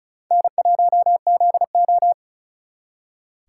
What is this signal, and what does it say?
Morse code, heard as a keyed tone, about 42 wpm, 700 Hz. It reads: N1ZO
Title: Speed Racing - USA 1x2 Call Signs 35wpm